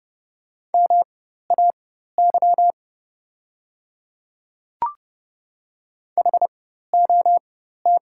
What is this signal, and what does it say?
Morse code, heard as a keyed tone, about 30 wpm, spaced as for 14 wpm, 700 Hz. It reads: MAY HOT
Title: 4x Character Spacing - Top 100 Words - 30wpm